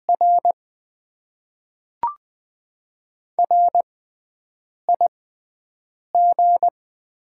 Morse code, heard as a keyed tone, about 20 wpm, spaced as for 7 wpm, 700 Hz. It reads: R RIG
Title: Top 20 QSO Elements - 6x Character Spacing - 20wpm